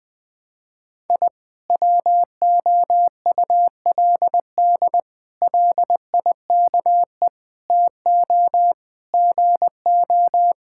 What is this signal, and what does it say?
Morse code, heard as a keyed tone, about 20 wpm, 700 Hz. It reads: I WOULD LIKE TO GO
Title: Sentences with 3-Word Phrases - 20wpm